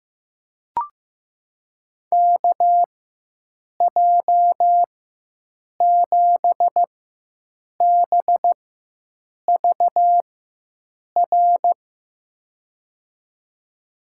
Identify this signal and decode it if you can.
Morse code, heard as a keyed tone, about 15 wpm, spaced as for 7 wpm, 700 Hz. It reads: KJ7BVR
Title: USA Call Signs - 6 Characters - 4x Character Spacing - 15wpm